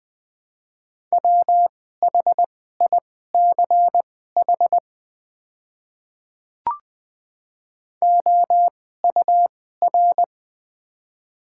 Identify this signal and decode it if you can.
Morse code, heard as a keyed tone, about 21 wpm, spaced as for 15 wpm, 700 Hz. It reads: WHICH OUR
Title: 2x Character Spacing - Top 200 Words - 20wpm